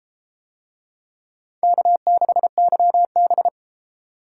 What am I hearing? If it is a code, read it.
Morse code, heard as a keyed tone, about 34 wpm, 700 Hz. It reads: K6YB